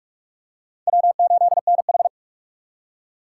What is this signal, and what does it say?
Morse code, heard as a keyed tone, about 46 wpm, 700 Hz. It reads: W8NH
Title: USA Call Signs - 4 Characters - 6x Character Spacing - 45wpm